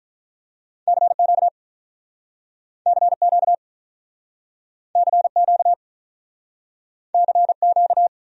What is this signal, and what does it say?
Morse code, heard as a keyed tone, about 49 wpm, 700 Hz. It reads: CQ CQ CQ CQ